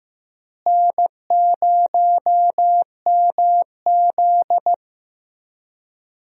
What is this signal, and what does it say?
Morse code, heard as a keyed tone, about 15 wpm, 700 Hz. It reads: N0MZ
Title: USA Call Signs - 4 Characters - Repeated 2x - 15wpm